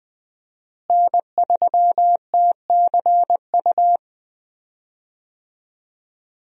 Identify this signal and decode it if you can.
Morse code, heard as a keyed tone, about 20 wpm, 700 Hz. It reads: N3TCU
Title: USA Call Signs - 5 Characters - Repeated 2x - 20wpm